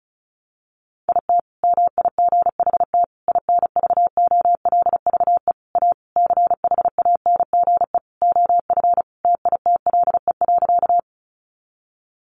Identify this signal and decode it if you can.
Morse code, heard as a keyed tone, about 34 wpm, 700 Hz. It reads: IT MIGHT INVOLVE A CHANGE OF TITLE.
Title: Sentences from Top 1200 Words 35wpm